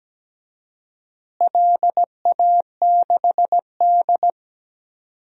Morse code, heard as a keyed tone, about 17 wpm, 700 Hz. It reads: LA6D
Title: DXCC-100 Call Signs - 4 Characters - Repeated 2x - 17wpm